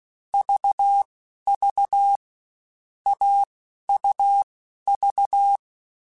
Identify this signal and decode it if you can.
Morse code, heard as a keyed tone, about 16 wpm, spaced as for 11 wpm, 800 Hz. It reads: VV AUV
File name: Difficult Combos / Difficult Combos-mp3s / AUV ShortRun @15